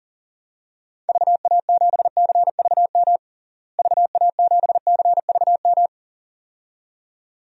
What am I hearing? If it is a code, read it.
Morse code, heard as a keyed tone, about 40 wpm, 700 Hz. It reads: VA7CVM VA7CVM
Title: DXCC-100 Call Signs - 6 Characters - Repeated 2x - 40wpm